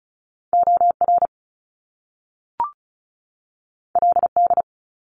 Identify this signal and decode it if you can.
Morse code, heard as a keyed tone, about 36 wpm, 700 Hz. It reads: OR LD
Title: Top Two Letter Combinations in Words 35wpm